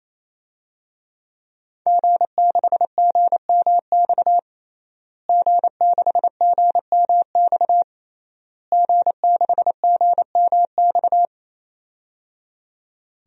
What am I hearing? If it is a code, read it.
Morse code, heard as a keyed tone, about 28 wpm, 700 Hz. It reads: G6GMX G6GMX G6GMX
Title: DXCC-100 Call Signs - 5 Characters - Repeated 3x - 28wpm